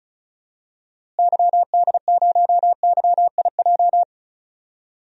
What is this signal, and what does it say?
Morse code, heard as a keyed tone, about 36 wpm, 700 Hz. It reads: YD0YIJ
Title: DXCC-100 Call Signs - 6 Characters - Repeated 3x - 35wpm